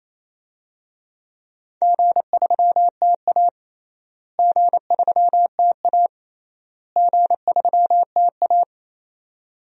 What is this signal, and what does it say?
Morse code, heard as a keyed tone, about 28 wpm, 700 Hz. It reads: G3TA G3TA G3TA
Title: DXCC-100 Call Signs - 4 Characters - Repeated 3x - 28wpm